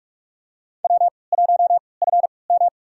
Morse code, heard as a keyed tone, about 46 wpm, spaced as for 26 wpm, 700 Hz. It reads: W1FM
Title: USA Call Signs - 4 Characters - 3x Character Spacing - 45wpm